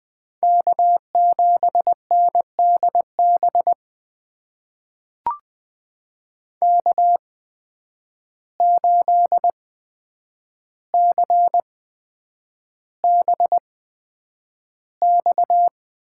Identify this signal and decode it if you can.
Morse code, heard as a keyed tone, about 20 wpm, spaced as for 9 wpm, 700 Hz. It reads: K7NDB K8CBX
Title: USA Call Signs - 8x Character Spacing - 20wpm